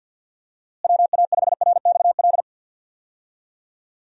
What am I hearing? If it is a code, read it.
Morse code, heard as a keyed tone, about 52 wpm, 700 Hz. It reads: WA5RXL
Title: USA Call Signs - 6 Characters - Repeated 3x - 50wpm